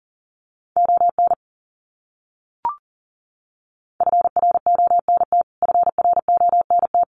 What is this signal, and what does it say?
Morse code, heard as a keyed tone, about 40 wpm, 700 Hz. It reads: ON FRONT FRONT
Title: Top 500 Words Repeated 40wpm